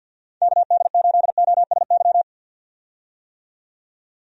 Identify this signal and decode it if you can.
Morse code, heard as a keyed tone, about 53 wpm, 700 Hz. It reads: KD8OSY
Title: USA Call Signs - 6 Characters - 50wpm